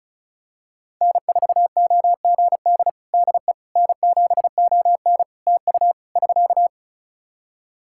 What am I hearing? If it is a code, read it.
Morse code, heard as a keyed tone, about 34 wpm, 700 Hz. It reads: N4OGD DE N7ON TU <SK>
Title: Practice QSO - Exchange 4 - Ending - 35wpm